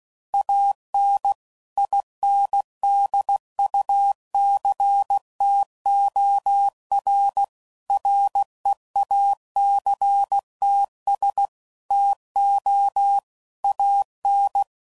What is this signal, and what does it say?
Morse code, heard as a keyed tone, about 16 wpm, 800 Hz. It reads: AN INDUCTOR REACTS TO AN
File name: CW Code Groups @15WPM / Technical Subjects / XL